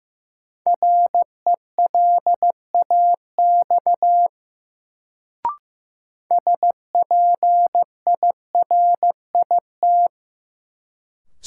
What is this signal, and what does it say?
Morse code, heard as a keyed tone, about 16 wpm, 700 Hz. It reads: RELAX SPIRIT
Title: Sentences from Top 1400 Words - New Word Drill - 15wpm